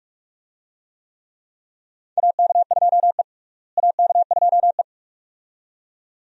Morse code, heard as a keyed tone, about 46 wpm, 700 Hz. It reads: AK2E AK2E
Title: USA Call Signs - 4 Characters - Repeated 2x - 45wpm